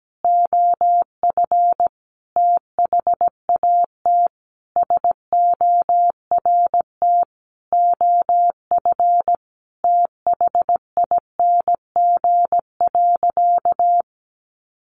Morse code, heard as a keyed tone, about 17 wpm, 700 Hz. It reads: OF THAT SORT OF THING.
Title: Sentences from Top 1300 Words 17wpm